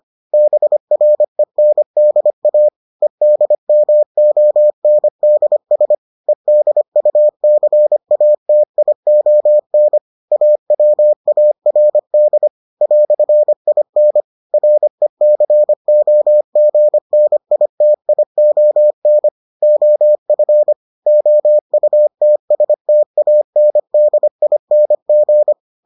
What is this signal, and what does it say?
Morse code, heard as a keyed tone, about 25 wpm, 600 Hz. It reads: BRENDA EDMONDS EDUCATION AWARD "IN RECOGNITION OF OUTSTANDING